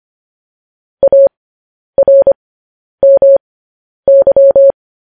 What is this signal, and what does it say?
Morse code, heard as a keyed tone, about 25 wpm, spaced as for 10 wpm, 550 Hz. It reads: ARMY
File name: podcast / News headlines-1x-25x10